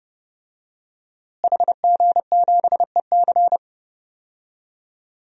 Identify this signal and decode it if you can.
Morse code, heard as a keyed tone, about 30 wpm, 700 Hz. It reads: HG7EC
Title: DXCC-100 Call Signs - 5 Characters - Repeated 3x - 30wpm